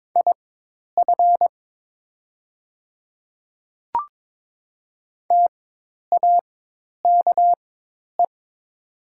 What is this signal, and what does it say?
Morse code, heard as a keyed tone, about 22 wpm, spaced as for 10 wpm, 700 Hz. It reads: IF TAKE
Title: 4x Character Spacing - Top 300 Words - 22wpm